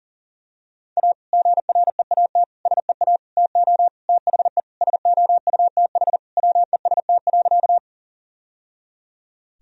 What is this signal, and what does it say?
Morse code, heard as a keyed tone, about 40 wpm, 700 Hz. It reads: A GREAT SEA TO THE SOUTH WEST.
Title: Sentences from Top 1000 Words - Review Part 2 - 40wpm